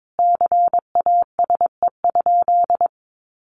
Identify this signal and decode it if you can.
Morse code, heard as a keyed tone, about 22 wpm, 700 Hz. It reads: CASE?
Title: Sentences from Top 1200 Words 22wpm